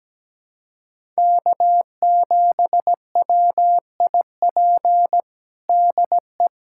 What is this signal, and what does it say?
Morse code, heard as a keyed tone, about 17 wpm, 700 Hz. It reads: K7WIP DE
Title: USA Callsign DE Callsign - 5 Characters - 17wpm